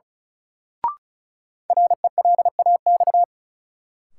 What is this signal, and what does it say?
Morse code, heard as a keyed tone, about 34 wpm, 700 Hz. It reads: RELAX
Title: Sentences from Top 1400 Words - New Word Drill - 35wpm